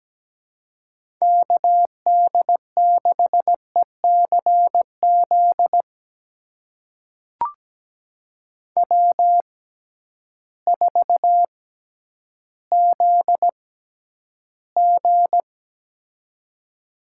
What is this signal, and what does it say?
Morse code, heard as a keyed tone, about 17 wpm, spaced as for 10 wpm, 700 Hz. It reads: KD6ECZ W4ZG
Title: USA Call Signs - 6x Character Spacing - 17wpm